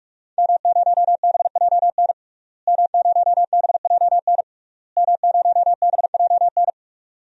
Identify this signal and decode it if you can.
Morse code, heard as a keyed tone, about 46 wpm, 700 Hz. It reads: M0BJN M0BJN M0BJN
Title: DXCC-100 Call Signs - 5 Characters - Repeated 3x - 45wpm